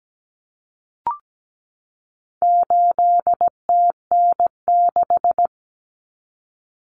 Morse code, heard as a keyed tone, about 17 wpm, 700 Hz. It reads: 8TN6
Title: Common Cut Numbers - 4-Digit 17wpm